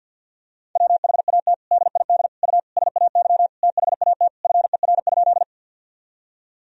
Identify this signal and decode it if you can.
Morse code, heard as a keyed tone, about 50 wpm, 700 Hz. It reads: WHAT DID U SAY THAT FER?
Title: Sentences from Top 200 Words and Abbreviations 50wpm